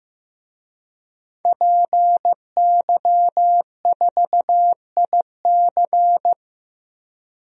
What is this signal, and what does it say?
Morse code, heard as a keyed tone, about 15 wpm, 700 Hz. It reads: PY4IC
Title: DXCC-100 Call Signs - 5 Characters - Repeated 2x - 15wpm